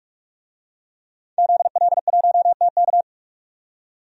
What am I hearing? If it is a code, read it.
Morse code, heard as a keyed tone, about 46 wpm, 700 Hz. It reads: ZL1TK
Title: DXCC-100 Call Signs - 5 Characters - 45wpm